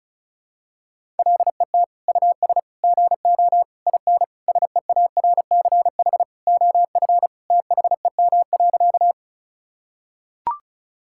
Morse code, heard as a keyed tone, about 36 wpm, 700 Hz. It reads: LET US GO IN SEARCH OF THEM.
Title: Sentences from Top 1000 Words - Review Part 5 - 35wpm